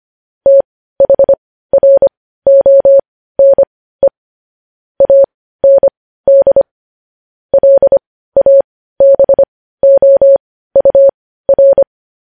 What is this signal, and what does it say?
Morse code, heard as a keyed tone, about 25 wpm, spaced as for 15 wpm, 550 Hz. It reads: THRONE AND LABOUR
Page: News Headlines in Morse Code at 15 WPM